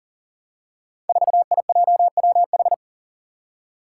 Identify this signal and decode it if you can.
Morse code, heard as a keyed tone, about 40 wpm, 700 Hz. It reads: 4IJWH
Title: Five Challenging Letter-Number Combinations 40wpm